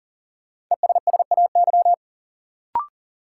Morse code, heard as a keyed tone, about 40 wpm, 700 Hz. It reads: ESSAY
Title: Sentences from Top 1500 Words - New Word Drill Alt - 40wpm